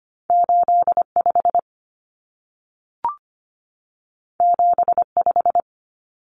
Morse code, heard as a keyed tone, about 25 wpm, 700 Hz. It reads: 85 75
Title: Common Cut Numbers - 2-Digit 25wpm